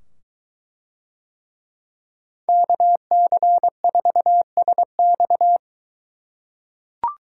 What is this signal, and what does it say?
Morse code, heard as a keyed tone, about 22 wpm, 700 Hz. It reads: KC4SX